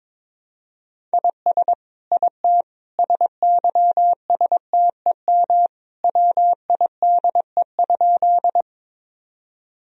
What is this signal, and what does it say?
Morse code, heard as a keyed tone, about 21 wpm, 700 Hz. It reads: IS IT SYSTEM WIDE?
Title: Sentences from Top 1000 Words - Review Part 3 - 22wpm